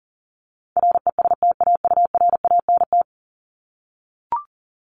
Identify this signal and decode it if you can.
Morse code, heard as a keyed tone, about 40 wpm, 700 Hz. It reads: RESTAURANT
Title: Top 1751-2000 Words 40wpm